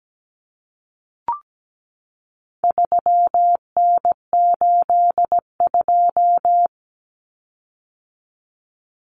Morse code, heard as a keyed tone, about 17 wpm, 700 Hz. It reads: 3N82
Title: Common Cut Numbers - 4-Digit 17wpm